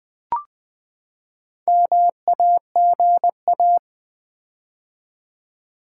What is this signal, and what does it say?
Morse code, heard as a keyed tone, about 20 wpm, 700 Hz. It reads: MAGA
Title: Build A Word - 4 Letters At A Time - 8-Letter Words - 20wpm